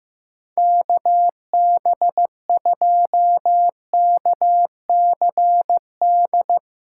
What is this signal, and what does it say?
Morse code, heard as a keyed tone, about 15 wpm, 700 Hz. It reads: KB2KCD
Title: USA Callsign DE Callsign 15wpm